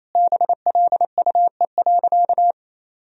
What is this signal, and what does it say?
Morse code, heard as a keyed tone, about 28 wpm, 700 Hz. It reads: BLUE.
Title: Sentences from Top 1000 Words - Review Part 3 - 28wpm